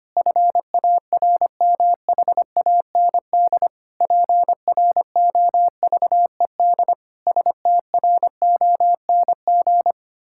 Morse code, heard as a keyed tone, about 25 wpm, 700 Hz. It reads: FARMHAND PROVED STRONG
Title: Pangram Challenge v2 - 25wpm